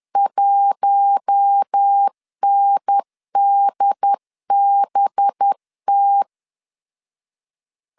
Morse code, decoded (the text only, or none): 1NDBT